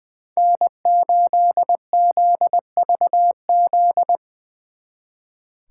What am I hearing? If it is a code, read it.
Morse code, heard as a keyed tone, about 22 wpm, 700 Hz. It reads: N8ZVZ
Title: Speed Racing - USA 1x3 Call Signs 20wpm